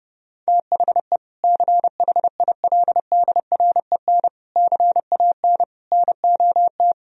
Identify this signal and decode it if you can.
Morse code, heard as a keyed tone, about 32 wpm, 700 Hz. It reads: THE CHILDREN CAN NOT
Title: Sentences from Top 1000 Words - Review Part 4 - 30wpm